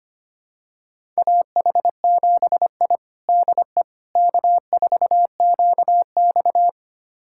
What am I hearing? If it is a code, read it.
Morse code, heard as a keyed tone, about 25 wpm, 700 Hz. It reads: AH7I DE K4QX
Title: USA Callsign DE Callsign - 4 Characters - 25wpm